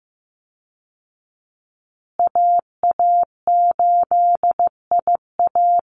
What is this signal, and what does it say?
Morse code, heard as a keyed tone, about 15 wpm, 700 Hz. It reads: AA8IA
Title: Call Signs Repeated 2x - 15wpm